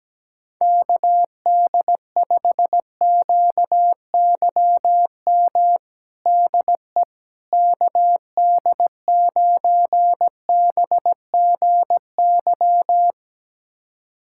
Morse code, heard as a keyed tone, about 17 wpm, 700 Hz. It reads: KD5QYM DE KD9BGY
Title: USA Callsign DE Callsign - 6 Characters - 17wpm